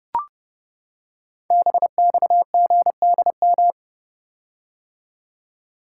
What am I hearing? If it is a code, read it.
Morse code, heard as a keyed tone, about 29 wpm, 700 Hz. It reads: BXGDM
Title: Code Groups - 5 Letters - 30wpm